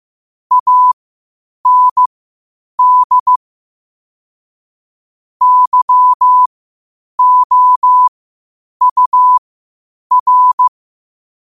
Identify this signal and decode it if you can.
Morse code, heard as a keyed tone, about 15 wpm, spaced as for 8 wpm, 1000 Hz. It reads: AND YOUR
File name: qotdfiles / 8wpm